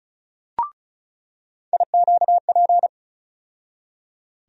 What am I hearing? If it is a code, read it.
Morse code, heard as a keyed tone, about 36 wpm, 700 Hz. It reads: IQP
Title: USA Call Sign Postfixes - 3 Letters 35wpm